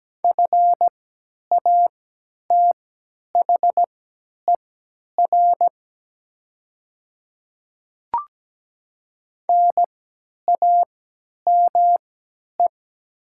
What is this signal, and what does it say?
Morse code, heard as a keyed tone, about 17 wpm, spaced as for 10 wpm, 700 Hz. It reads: FATHER NAME
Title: 3x Character Spacing - Top 200 Words - 17wpm